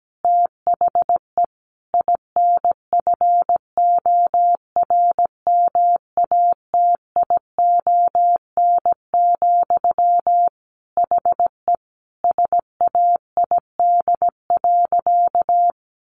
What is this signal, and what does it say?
Morse code, heard as a keyed tone, about 17 wpm, 700 Hz. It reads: THE INFORMATION, HE SAID.
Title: Sentences from Top 600 Words 17wpm